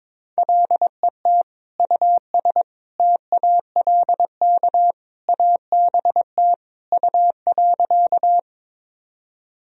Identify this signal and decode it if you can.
Morse code, heard as a keyed tone, about 22 wpm, 700 Hz. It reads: LET US TALK ABT U.
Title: Sentences from Top 400 Words and Abbreviations 22wpm